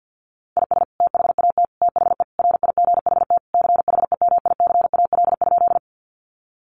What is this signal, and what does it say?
Morse code, heard as a keyed tone, about 50 wpm, 700 Hz. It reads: IS THAT THE RIGHT CHEMICAL?
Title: Sentences from Top 1200 Words 50wpm